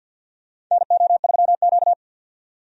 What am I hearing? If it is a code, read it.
Morse code, heard as a keyed tone, about 52 wpm, 700 Hz. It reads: NO3Q
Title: USA Call Signs - 4 Characters - 50wpm